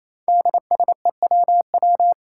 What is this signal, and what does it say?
Morse code, heard as a keyed tone, about 28 wpm, 700 Hz. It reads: DSEWW
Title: Five Challenging Letter-Number Combinations 28wpm